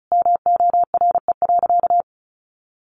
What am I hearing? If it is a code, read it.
Morse code, heard as a keyed tone, about 34 wpm, 700 Hz. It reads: MORE.
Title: Sentences from Top 800 Words 35wpm